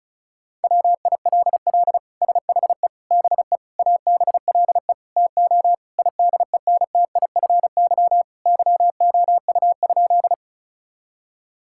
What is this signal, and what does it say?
Morse code, heard as a keyed tone, about 36 wpm, 700 Hz. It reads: WILL SHE BE ABLE TO IDENTIFY YOU?
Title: Sentences from Top 1000 Words - Review Part 6 - 35wpm